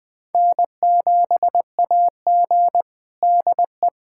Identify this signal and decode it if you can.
Morse code, heard as a keyed tone, about 20 wpm, 700 Hz. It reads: N7AG DE
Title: USA Callsign DE Callsign - 4 Characters - 20wpm